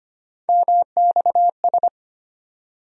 Morse code, heard as a keyed tone, about 25 wpm, 700 Hz. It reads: MXS
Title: USA Call Sign Postfixes - 3 Letters 25wpm